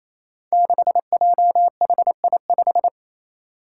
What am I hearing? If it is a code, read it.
Morse code, heard as a keyed tone, about 28 wpm, 700 Hz. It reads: 6JHI5
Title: Five Challenging Letter-Number Combinations 28wpm